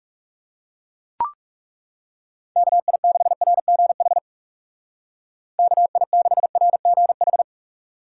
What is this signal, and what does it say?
Morse code, heard as a keyed tone, about 36 wpm, 700 Hz. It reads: KI6RGH KI6RGH
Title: Speed Racing - USA 2x3 Call Signs 30wpm